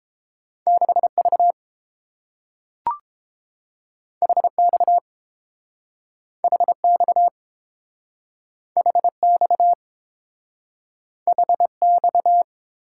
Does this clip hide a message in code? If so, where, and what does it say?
Yes: Morse code from 0.7 to 12.4 s, heard as a keyed tone, about 30 wpm, 700 Hz. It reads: 6V HX HX HX HX